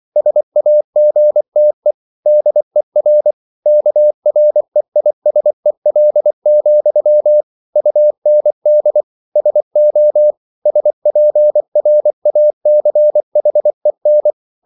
Morse code, heard as a keyed tone, about 24 wpm, 600 Hz. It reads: SAGTE DER KREISEL, UND SO SPRACHEN